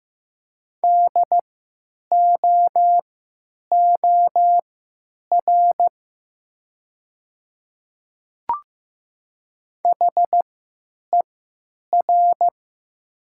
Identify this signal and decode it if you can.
Morse code, heard as a keyed tone, about 15 wpm, spaced as for 9 wpm, 700 Hz. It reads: DOOR HER
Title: 3x Character Spacing - Top 300 Words - 15wpm